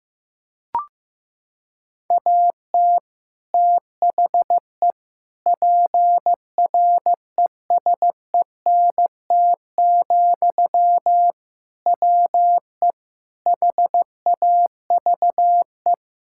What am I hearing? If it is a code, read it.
Morse code, heard as a keyed tone, about 15 wpm, 700 Hz. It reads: AT THE PRESENT, WE HAVE
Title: Sentences from Top 1000 Words - Review Part 5 - 15wpm